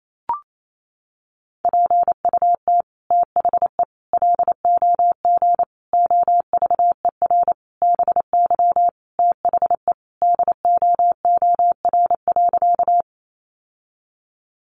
Morse code, heard as a keyed tone, about 28 wpm, 700 Hz. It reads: PUT THE LOG OVER BY THE DOOR.
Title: Sentences from Top 1400 Words 28wpm